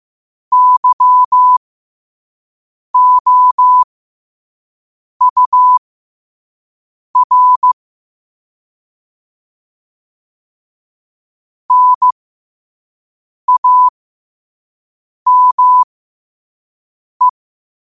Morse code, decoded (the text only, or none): YOUR NAME